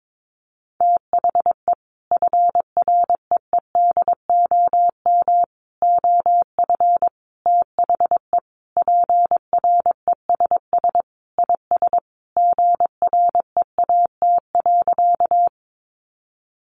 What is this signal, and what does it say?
Morse code, heard as a keyed tone, about 22 wpm, 700 Hz. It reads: THE FREEDOM OF THE PRESS IS GREAT.
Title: Sentences from Top 800 Words 22wpm